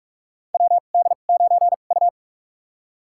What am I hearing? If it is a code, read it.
Morse code, heard as a keyed tone, about 46 wpm, spaced as for 32 wpm, 690 Hz. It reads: WD9U
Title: USA Call Signs - 4 Characters - 2x Character Spacing - 45wpm